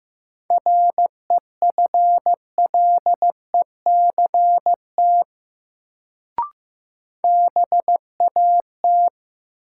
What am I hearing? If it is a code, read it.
Morse code, heard as a keyed tone, about 15 wpm, 700 Hz. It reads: REFLECT BAT
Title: Sentences from Top 1500 Words - New Word Drill - 15wpm